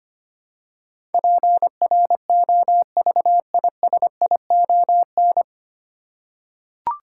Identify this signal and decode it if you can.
Morse code, heard as a keyed tone, about 25 wpm, 700 Hz. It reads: PROVISION
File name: practice / 25wpm / 9 Letter Words - First 3 Letters Given - 25wpm